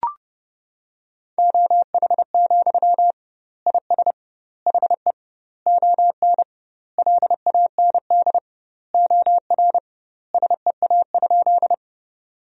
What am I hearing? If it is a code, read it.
Morse code, heard as a keyed tone, about 32 wpm, 700 Hz. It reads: OH, IS HE ON LAND OR SEA?
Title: 2x Word-Spacing - Sentences from Top 500 Words - 30wpm